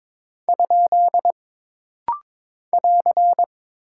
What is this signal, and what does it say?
Morse code, heard as a keyed tone, about 22 wpm, 700 Hz. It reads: ? <AR>
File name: practice / 22wpm / Single Punctuation-Prosign - Rapid-Fire - 22wpm